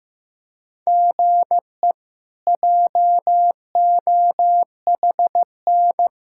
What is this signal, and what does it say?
Morse code, heard as a keyed tone, about 16 wpm, 700 Hz. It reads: GE JOHN
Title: K1USN SST QSO Practice - 15wpm